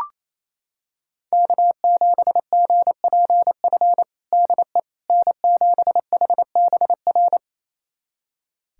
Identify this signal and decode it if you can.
Morse code, heard as a keyed tone, about 28 wpm, 700 Hz. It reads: K7GPF DE N7HBR
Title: USA Callsign DE Callsign - 5 Characters - 28wpm